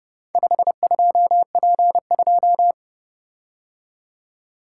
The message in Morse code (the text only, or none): 52P2